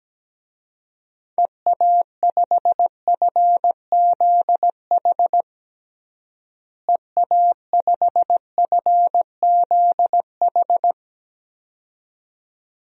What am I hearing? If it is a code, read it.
Morse code, heard as a keyed tone, about 17 wpm, 700 Hz. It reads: EA5FZH EA5FZH
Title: DXCC-100 Call Signs - 6 Characters - Repeated 2x - 17wpm